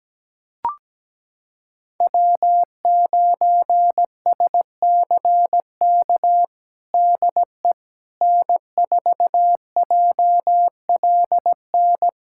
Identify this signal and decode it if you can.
Morse code, heard as a keyed tone, about 17 wpm, 700 Hz. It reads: W9SCK DE N4JLN
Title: USA Callsign DE Callsign - 5 Characters - 17wpm